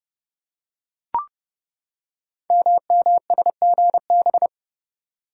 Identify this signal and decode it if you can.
Morse code, heard as a keyed tone, about 31 wpm, 700 Hz. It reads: MMSGB
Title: Code Groups - 5 Letters - 30wpm